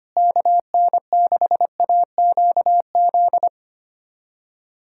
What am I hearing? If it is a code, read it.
Morse code, heard as a keyed tone, about 25 wpm, 700 Hz. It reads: KN6AQZ
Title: USA Callsign DE Callsign 25wpm